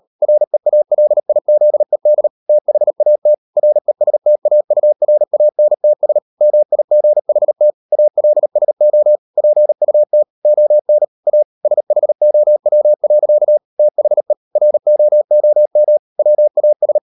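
Morse code, heard as a keyed tone, about 38 wpm, 600 Hz. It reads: REALIZED THAT RESTAURANTS MIGHT ALSO PUT ON A SHOW. THE ROOM WAS
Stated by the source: Morse code practice at 38 WPM, sending Ken Follett's novel Never in English, from CW med Gnister.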